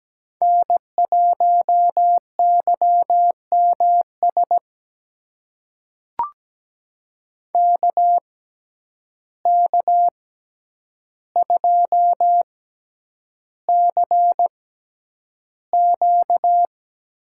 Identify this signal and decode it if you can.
Morse code, heard as a keyed tone, about 17 wpm, spaced as for 9 wpm, 700 Hz. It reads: N1YMS KK2CQ
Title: USA Call Signs - 5 Characters - 6x Character Spacing - 17wpm